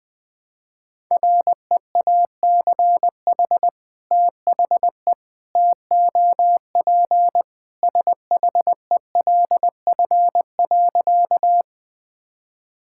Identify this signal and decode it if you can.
Morse code, heard as a keyed tone, about 20 wpm, 700 Hz. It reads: REACH THE TOP SHELF.
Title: Sentences with 2-Word Phrases - Spoken Beginning - 20wpm